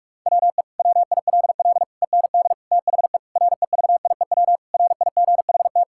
Morse code, heard as a keyed tone, about 46 wpm, 700 Hz. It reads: WE WILL END THE REVIEW RIGHT
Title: Sentences from Top 1000 Words - Review Part 6 - 45wpm